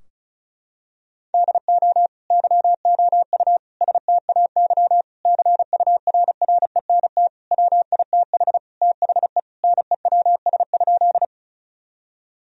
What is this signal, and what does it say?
Morse code, heard as a keyed tone, about 34 wpm, 700 Hz. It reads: DO YOU STAY CURRENT WITH THE NEWS?
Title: Sentences from Top 1000 Words - Review Part 4 - 35wpm